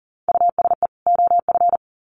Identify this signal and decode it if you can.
Morse code, heard as a keyed tone, about 40 wpm, 700 Hz. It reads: USE OF
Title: Sets of 2 Words 40wpm